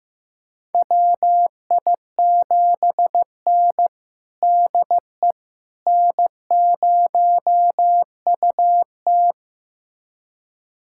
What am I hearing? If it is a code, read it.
Morse code, heard as a keyed tone, about 16 wpm, 700 Hz. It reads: WI7N DE N0UT
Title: USA Callsign DE Callsign - 4 Characters - 15wpm